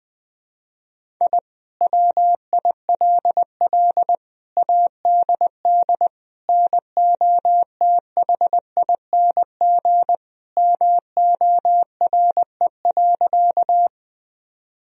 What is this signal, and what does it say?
Morse code, heard as a keyed tone, about 20 wpm, 700 Hz. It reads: I WILL ADD NOTHING MORE.
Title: Sentences from Top 1000 Words - Review Part 1 - 20wpm